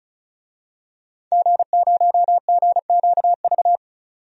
Morse code, heard as a keyed tone, about 35 wpm, 700 Hz. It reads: G0GQV
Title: DXCC-100 Call Signs - 5 Characters - 35wpm